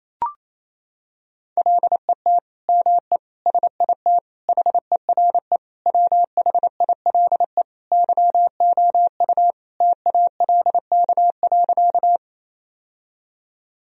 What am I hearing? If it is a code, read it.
Morse code, heard as a keyed tone, about 28 wpm, 700 Hz. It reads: LET ME SIT HERE WHILE YOU TALK.
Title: Sentences from Top 1000 Words - Review Part 1 - 28wpm